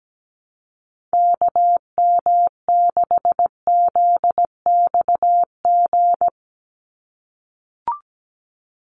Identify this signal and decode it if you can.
Morse code, heard as a keyed tone, about 17 wpm, 700 Hz. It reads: KM6ZXG
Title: USA Call Signs 17wpm